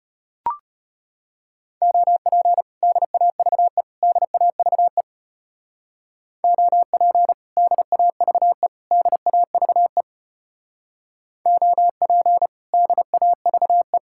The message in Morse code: OP DAVE DAVE OP DAVE DAVE OP DAVE

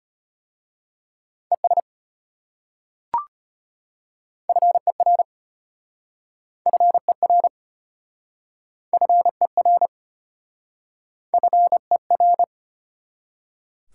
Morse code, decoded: ES FER FER FER FER

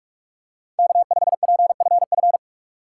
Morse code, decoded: K5PFF

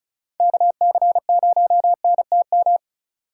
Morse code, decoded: KC0NTM

